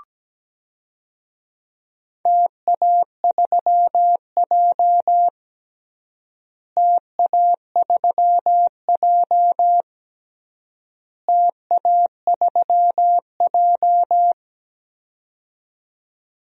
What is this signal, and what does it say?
Morse code, heard as a keyed tone, about 17 wpm, 700 Hz. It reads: TA3J TA3J TA3J
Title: DXCC-100 Call Signs - 4 Characters - Repeated 3x - 17wpm